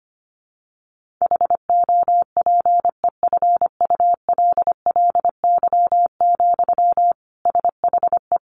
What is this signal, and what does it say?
Morse code, heard as a keyed tone, about 25 wpm, 700 Hz. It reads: HOPEFULLY, SHE
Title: Sentences from Top 1500 Words 25wpm